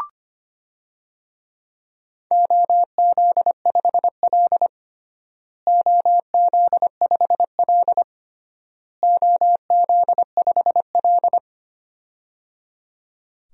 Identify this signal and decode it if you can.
Morse code, heard as a keyed tone, about 25 wpm, 700 Hz. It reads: OZ5L OZ5L OZ5L
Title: DXCC-100 Call Signs - 4 Characters - Repeated 3x - 25wpm